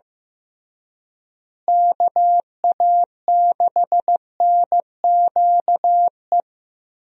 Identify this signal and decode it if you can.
Morse code, heard as a keyed tone, about 15 wpm, 700 Hz. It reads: KA6NQE
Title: USA Call Signs - 6 Characters - Repeated 3x - 15wpm